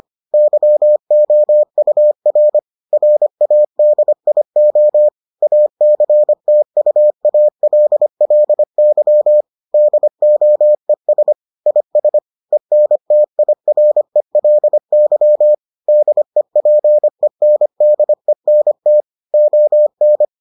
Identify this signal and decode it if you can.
Morse code, heard as a keyed tone, about 25 wpm, 600 Hz. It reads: YOUR RADIO ACTUALLY DOES IS ENTIRELY DEPENDENT ON